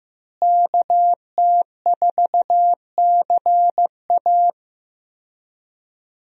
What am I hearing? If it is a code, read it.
Morse code, heard as a keyed tone, about 15 wpm, 700 Hz. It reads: KT4CA
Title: USA Call Signs - 5 Characters - Repeated 2x - 15wpm